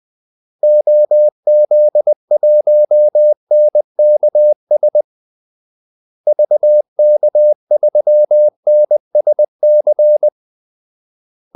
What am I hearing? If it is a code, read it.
Morse code, heard as a keyed tone, about 20 wpm, 600 Hz. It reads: OZ1NKS VK3NSC
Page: Kaldesignaler 20 wpm